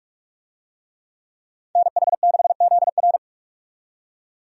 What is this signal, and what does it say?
Morse code, heard as a keyed tone, about 41 wpm, 700 Hz. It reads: NH6ZR
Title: USA Call Signs - 5 Characters - 45wpm